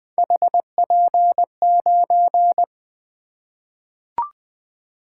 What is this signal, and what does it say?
Morse code, heard as a keyed tone, about 21 wpm, 700 Hz. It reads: HP9
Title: Three Letter-Number Combinations - 20wpm